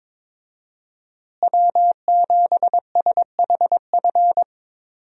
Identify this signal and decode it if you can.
Morse code, heard as a keyed tone, about 21 wpm, 700 Hz. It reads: W7SHF
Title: USA Call Signs - 5 Characters - Repeated 2x - 22wpm